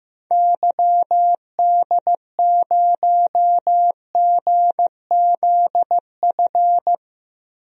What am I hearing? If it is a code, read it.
Morse code, heard as a keyed tone, about 15 wpm, 700 Hz. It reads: YD0GZF
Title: DXCC-100 Call Signs - 6 Characters - Repeated 3x - 15wpm